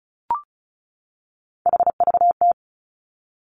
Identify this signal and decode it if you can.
Morse code, heard as a keyed tone, about 36 wpm, 700 Hz. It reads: HVT
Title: Three Challenging Letter-Number Combinations 35wpm